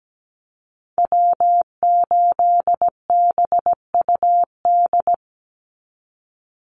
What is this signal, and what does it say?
Morse code, heard as a keyed tone, about 17 wpm, 700 Hz. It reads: W8BUD
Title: Call Signs Repeated 2x - 17wpm